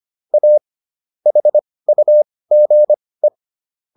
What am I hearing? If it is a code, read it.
Morse code, heard as a keyed tone, about 25 wpm, spaced as for 18 wpm, 600 Hz. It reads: A HUGE